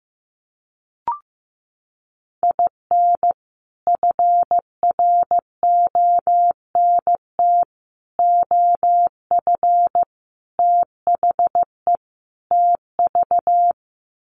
Morse code, heard as a keyed tone, about 15 wpm, 700 Hz. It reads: IN FRONT OF THE TV
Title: Sets of 5 Words Encore2 15wpm